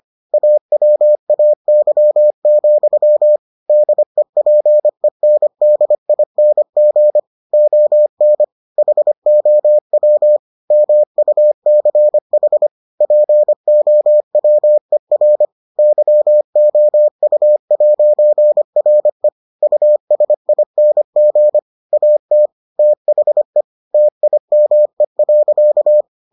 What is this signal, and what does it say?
Morse code, heard as a keyed tone, about 25 wpm, 600 Hz. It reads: AWAY, DEPENDING ON HOW MUCH POWER YOU'RE USING AT THE TIME.